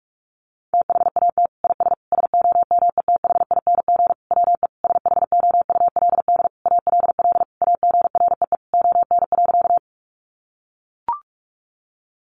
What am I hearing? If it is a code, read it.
Morse code, heard as a keyed tone, about 44 wpm, 700 Hz. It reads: THAT IS SOMETHING WE SHOULD ALL AGREE ON.
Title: Sentences from Top 1100 Words 45wpm